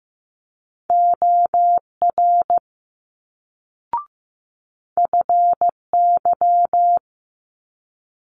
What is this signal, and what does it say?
Morse code, heard as a keyed tone, about 15 wpm, 700 Hz. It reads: OR FY
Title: Multi-letter Phonemes Quick Fire 15wpm